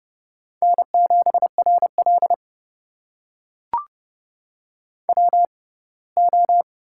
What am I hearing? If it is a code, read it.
Morse code, heard as a keyed tone, about 30 wpm, spaced as for 20 wpm, 700 Hz. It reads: N7RL WO
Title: USA Call Signs - 4 Characters - 6x Character Spacing - 30wpm